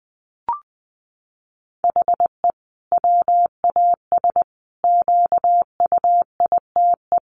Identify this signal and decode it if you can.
Morse code, heard as a keyed tone, about 20 wpm, 700 Hz. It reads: HE WAS QUITE
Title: Sentences from Top 1400 Words 20wpm